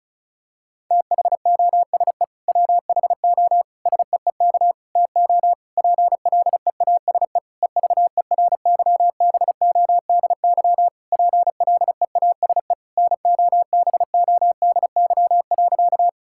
Morse code, heard as a keyed tone, about 34 wpm, 700 Hz. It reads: THOSE WHO SEEK TO PLEASE EVERYBODY PLEASE NOBODY.
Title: Sentences from Top 1600 Words 35wpm